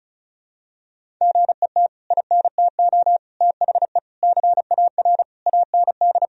Morse code, heard as a keyed tone, about 36 wpm, 700 Hz. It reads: GET INTO THE CAR AND
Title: Sentences from Top 1000 Words - Review Part 1 - 35wpm